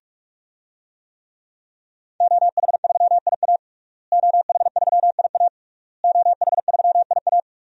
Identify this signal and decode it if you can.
Morse code, heard as a keyed tone, about 46 wpm, 700 Hz. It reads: OH3IA OH3IA OH3IA
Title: DXCC-100 Call Signs - 5 Characters - Repeated 3x - 45wpm